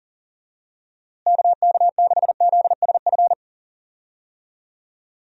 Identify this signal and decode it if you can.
Morse code, heard as a keyed tone, about 40 wpm, 700 Hz. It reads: KK6ZSF
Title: USA Call Signs - 6 Characters - Repeated 2x - 40wpm